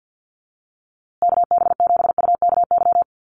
Morse code, heard as a keyed tone, about 50 wpm, 700 Hz. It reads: KB7UKY
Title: USA Call Signs Repeated 2x 50wpm